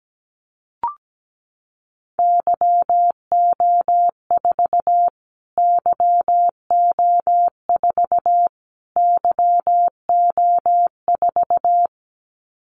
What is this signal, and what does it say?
Morse code, heard as a keyed tone, about 17 wpm, 700 Hz. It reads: YO4 YO4 YO4
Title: Callsign Prefixes for Top 25 DXCC - Repeated 3x - 17wpm